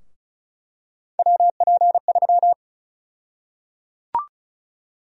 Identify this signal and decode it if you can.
Morse code, heard as a keyed tone, about 36 wpm, 700 Hz. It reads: WP3
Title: USA Call Sign Prefixes - Repeated 2x - 35wpm